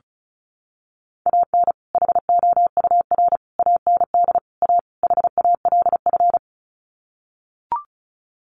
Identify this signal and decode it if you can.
Morse code, heard as a keyed tone, about 34 wpm, 700 Hz. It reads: AN HOUR AND A HALF
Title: Sets of 5 Words 35wpm